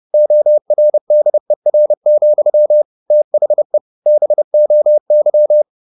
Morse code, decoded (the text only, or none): ORDER, THE BOY